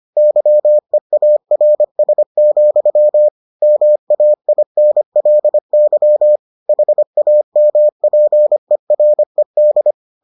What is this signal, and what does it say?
Morse code, heard as a keyed tone, about 25 wpm, 600 Hz. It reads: YEARS, MAINLY HAMPERED